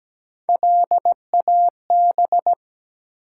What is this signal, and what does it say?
Morse code, heard as a keyed tone, about 17 wpm, 700 Hz. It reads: LAB
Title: Three Letter Words - 17wpm